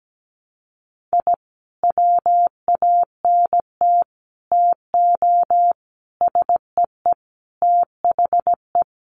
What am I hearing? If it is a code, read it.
Morse code, heard as a keyed tone, about 17 wpm, 700 Hz. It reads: I WANT TO SEE THE
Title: Sentences from Top 1300 Words 17wpm